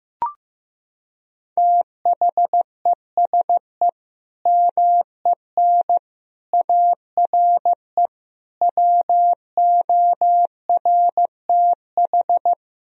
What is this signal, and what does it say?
Morse code, heard as a keyed tone, about 15 wpm, 700 Hz. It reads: THESE MEN ARE WORTH
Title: Sentences from Top 1000 Words - Review Part 1 - 15wpm